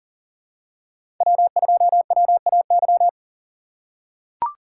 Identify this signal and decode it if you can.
Morse code, heard as a keyed tone, about 40 wpm, 700 Hz. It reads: W2WAY
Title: USA Call Signs - 5 Characters - 2x Character Spacing - 40wpm